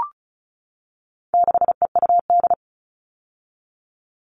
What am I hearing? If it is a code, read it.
Morse code, heard as a keyed tone, about 34 wpm, 700 Hz. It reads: 6EUD
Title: All Cut Numbers - 4-Digit 35wpm